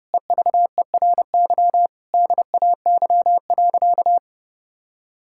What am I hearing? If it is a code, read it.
Morse code, heard as a keyed tone, about 32 wpm, 700 Hz. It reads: EVERY DAY.
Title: Sentences from Top 1600 Words 30wpm